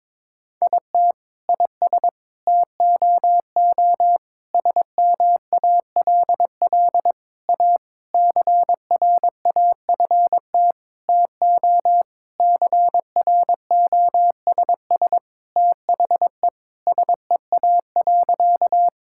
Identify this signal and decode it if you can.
Morse code, heard as a keyed tone, about 21 wpm, 700 Hz. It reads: IT IS TOO SMALL A CRAFT TO CROSS THE SEA.
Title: Sentences from Top 1000 Words - Review Part 6 - 22wpm